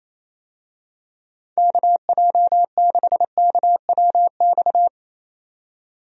Morse code, heard as a keyed tone, about 28 wpm, 700 Hz. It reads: KJ6KWX
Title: USA Call Signs - 6 Characters - 28wpm